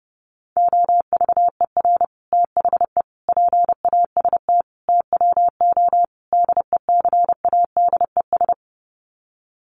Morse code, heard as a keyed tone, about 30 wpm, 700 Hz. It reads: OVER THE PAST TWO DECADES
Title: Sets of 5 Words Encore2 30wpm